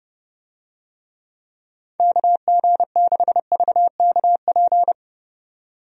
Morse code, heard as a keyed tone, about 32 wpm, 700 Hz. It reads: KG6VKP
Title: USA Call Signs - 6 Characters - 30wpm